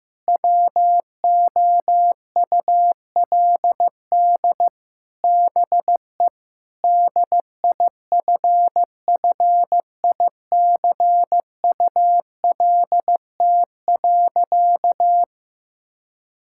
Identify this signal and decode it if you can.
Morse code, heard as a keyed tone, about 16 wpm, 700 Hz. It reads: WOULD BE DIFFICULT.
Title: Sentences from Top 1600 Words 15wpm